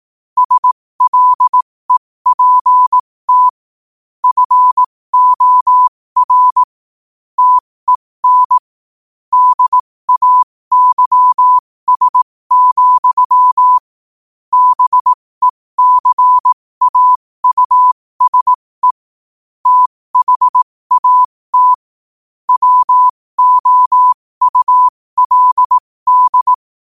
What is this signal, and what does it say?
Morse code, heard as a keyed tone, about 18 wpm, spaced as for 15 wpm, 1000 Hz. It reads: SLEPT FOR TEN DAYS, BECAUSE THAT WOULD
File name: qotdfiles / 15wpm